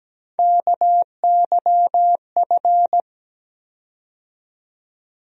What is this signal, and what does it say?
Morse code, heard as a keyed tone, about 17 wpm, 700 Hz. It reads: KYF
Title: USA Call Sign Postfixes - 3 Letters 17wpm